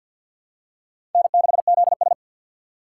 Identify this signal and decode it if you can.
Morse code, heard as a keyed tone, about 47 wpm, 700 Hz. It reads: N6ZS
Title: USA Call Signs - 4 Characters - 2x Character Spacing - 50wpm